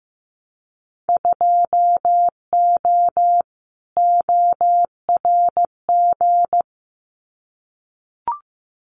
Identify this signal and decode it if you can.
Morse code, heard as a keyed tone, about 15 wpm, 700 Hz. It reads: 2O ORG
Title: Winter Field Day Exchanges 15wpm